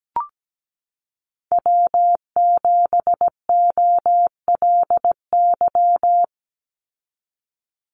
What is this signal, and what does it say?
Morse code, heard as a keyed tone, about 17 wpm, 700 Hz. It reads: W7OLY
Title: Call Signs Repeated 3x - 17wpm